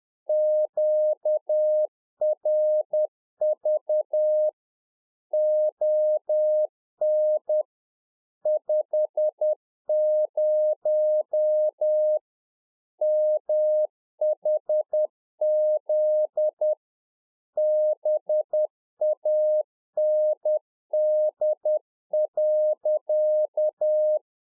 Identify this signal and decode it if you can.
Morse code, heard as a keyed tone, about 10 wpm, 610 Hz. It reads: QRV ON 50 MHZ BAND.
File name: 2011-04-mp3 / RICK JJ0MPI 10WPM